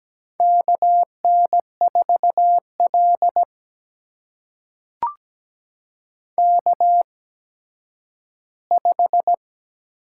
Morse code, heard as a keyed tone, about 17 wpm, spaced as for 10 wpm, 700 Hz. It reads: KN4L K5
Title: USA Call Signs - 4 Characters - 8x Character Spacing - 17wpm